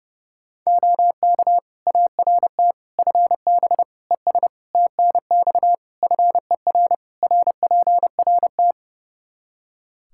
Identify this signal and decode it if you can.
Morse code, heard as a keyed tone, about 30 wpm, 700 Hz. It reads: OK ART FB ES TNX FER RPRT
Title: Practice QSO - Exchange 2 - Info - 30wpm